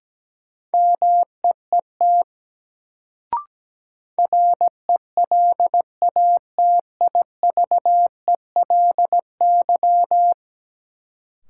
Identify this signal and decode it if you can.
Morse code, heard as a keyed tone, about 17 wpm, 700 Hz. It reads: MEET RELATIVELY
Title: Sentences from Top 1100 Words - New Word Drill Alt - 17wpm